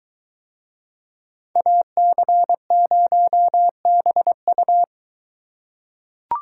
Morse code, heard as a keyed tone, about 23 wpm, 700 Hz. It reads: AC0BU